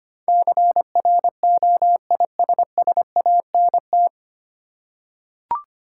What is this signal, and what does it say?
Morse code, heard as a keyed tone, about 25 wpm, 690 Hz. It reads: CROISSANT
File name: practice / 25wpm / Easy Words with Hints - 25wpm